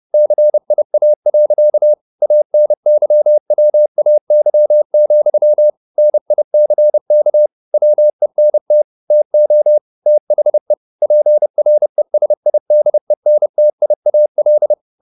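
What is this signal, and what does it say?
Morse code, heard as a keyed tone, about 30 wpm, 600 Hz. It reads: CIA. ANYWAY, NICK WENT TO THE PRESIDENTIAL
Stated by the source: Morse code at 30 wpm, the novel Never by Ken Follett in English, from CW med Gnister.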